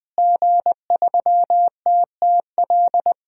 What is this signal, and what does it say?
Morse code, heard as a keyed tone, about 20 wpm, 700 Hz. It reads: G3TTL
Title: DXCC-100 Call Signs - 5 Characters - Repeated 3x - 20wpm